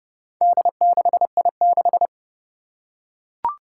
Morse code, heard as a keyed tone, about 30 wpm, 700 Hz. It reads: D6I6
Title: Four Challenging Letter-Number Combinations 30wpm